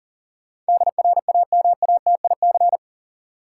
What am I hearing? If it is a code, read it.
Morse code, heard as a keyed tone, about 40 wpm, 700 Hz. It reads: DRAMATIC
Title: Sentences from Top 1300 Words - New Word Drill - 40wpm